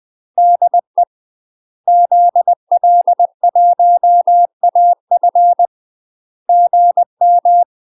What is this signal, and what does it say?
Morse code, heard as a keyed tone, about 20 wpm, 700 Hz. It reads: DE ZL1AF GM
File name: UK MASTER QSOs TRAINING / 20WPM / UK MASTER QSOs 0031